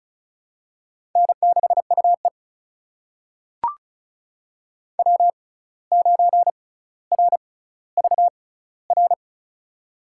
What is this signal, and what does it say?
Morse code, heard as a keyed tone, about 36 wpm, spaced as for 17 wpm, 700 Hz. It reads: N6UE W9RVR
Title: USA Call Signs - 6x Character Spacing - 35wpm